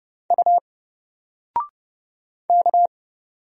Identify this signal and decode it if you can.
Morse code, heard as a keyed tone, about 30 wpm, 700 Hz. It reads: U K
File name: Single-Letter-Number-30wpm